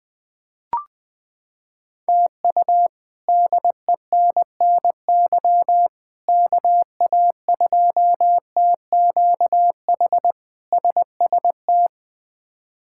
Morse code, heard as a keyed tone, about 20 wpm, 700 Hz. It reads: TU DENNY KA2TQH SST
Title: K1USN SST QSO Practice - 20wpm